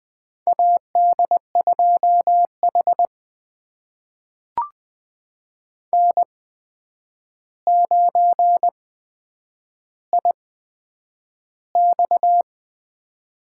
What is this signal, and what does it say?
Morse code, heard as a keyed tone, about 21 wpm, spaced as for 9 wpm, 700 Hz. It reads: AD2H N9IX
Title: USA Call Signs - 4 Characters - 8x Character Spacing - 20wpm